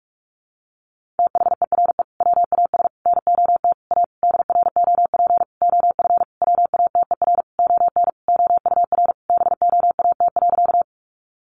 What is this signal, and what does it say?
Morse code, heard as a keyed tone, about 44 wpm, 700 Hz. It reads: THERE WAS NOT A DROP OF WATER ON OUR BOAT.
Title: Sentences from Top 1400 Words 45wpm